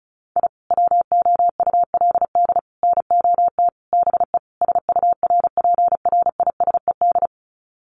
Morse code, heard as a keyed tone, about 34 wpm, 700 Hz. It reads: I WOULD NOT BE SURPRISED
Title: Sets of 5 Words Encore2 35wpm